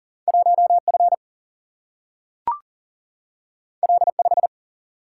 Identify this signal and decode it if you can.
Morse code, heard as a keyed tone, about 42 wpm, 700 Hz. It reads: 1F L5
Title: Single-Letter Single-Number 40wpm